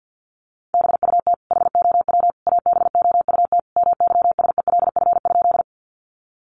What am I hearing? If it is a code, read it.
Morse code, heard as a keyed tone, about 50 wpm, 700 Hz. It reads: BUT HOW ABOUT MYSELF?
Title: Sentences from Top 1300 Words 50wpm